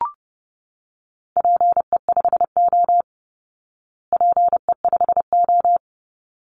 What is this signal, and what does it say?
Morse code, heard as a keyed tone, about 30 wpm, 700 Hz. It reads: PE5O PE5O
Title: Call Signs Repeated 3x - 30wpm